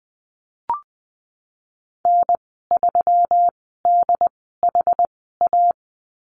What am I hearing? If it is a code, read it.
Morse code, heard as a keyed tone, about 20 wpm, spaced as for 15 wpm, 700 Hz. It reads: N3DHA
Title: 161 Lesson 26g - Letter g - Call Signs 20wpm